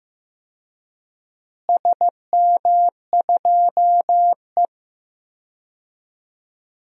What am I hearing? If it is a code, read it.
Morse code, heard as a keyed tone, about 16 wpm, 700 Hz. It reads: SM2E
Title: DXCC-100 Call Signs - 4 Characters - Repeated 2x - 15wpm